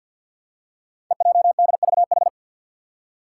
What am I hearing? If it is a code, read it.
Morse code, heard as a keyed tone, about 52 wpm, 700 Hz. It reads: EJDVH
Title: Five Challenging Letter-Number Combinations 50wpm